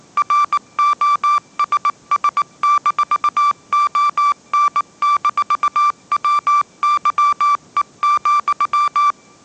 Morse code, heard as a keyed tone, about 19 wpm, 1170 Hz. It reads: ROSS-ON-WYE,
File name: 20wpm